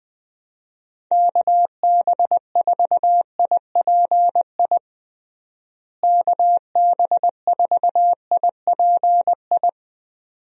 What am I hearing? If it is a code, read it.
Morse code, heard as a keyed tone, about 20 wpm, 700 Hz. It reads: KB4IPI KB4IPI
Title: USA Call Signs - 6 Characters - Repeated 2x - 20wpm